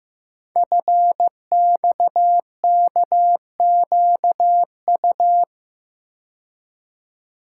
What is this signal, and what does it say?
Morse code, heard as a keyed tone, about 15 wpm, 700 Hz. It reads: FXKQU
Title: Code Groups - 5 Letters - 15wpm